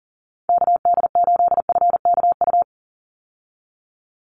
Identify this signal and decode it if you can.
Morse code, heard as a keyed tone, about 38 wpm, 700 Hz. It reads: KD8FKU
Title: USA Call Signs 40wpm